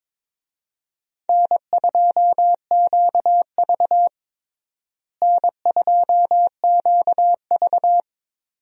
Morse code, heard as a keyed tone, about 23 wpm, 700 Hz. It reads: N2QV N2QV
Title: USA Call Signs - 4 Characters - Repeated 2x - 22wpm